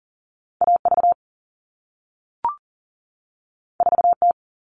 Morse code, heard as a keyed tone, about 42 wpm, 700 Hz. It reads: AV 4T
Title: All Cut Numbers - 2-Digit 40wpm